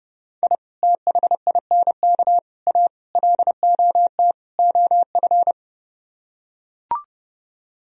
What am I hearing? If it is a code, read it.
Morse code, heard as a keyed tone, about 30 wpm, 700 Hz. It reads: I THINK A LOT OF